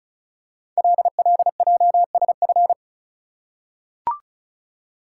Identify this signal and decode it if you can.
Morse code, heard as a keyed tone, about 36 wpm, 700 Hz. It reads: LLJSF
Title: Code Groups - 5 Letters - 35wpm